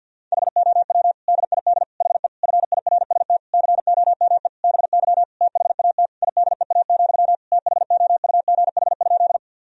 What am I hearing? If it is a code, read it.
Morse code, heard as a keyed tone, about 53 wpm, 700 Hz. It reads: HOW DID HE FIRST COME BY THAT IDEA, THOUGH?
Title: Sentences from Top 1000 Words - Review Part 2 - 50wpm